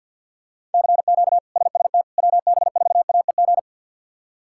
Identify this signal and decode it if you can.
Morse code, heard as a keyed tone, about 47 wpm, 700 Hz. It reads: CQ SST WB4AEG
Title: K1USN SST QSO Practice - 50wpm